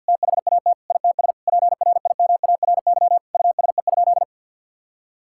Morse code, heard as a keyed tone, about 50 wpm, 700 Hz. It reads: THAT ITS PRIMARY USE?
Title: Sentences from Top 1000 Words - Review Part 5 - 50wpm